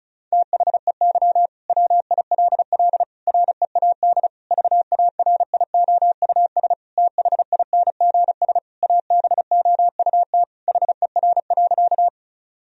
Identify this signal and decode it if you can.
Morse code, heard as a keyed tone, about 36 wpm, 700 Hz. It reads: THEY WILL READ VARIOUS THINGS ABOUT HER.
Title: Sentences from Top 1000 Words - Review Part 4 - 35wpm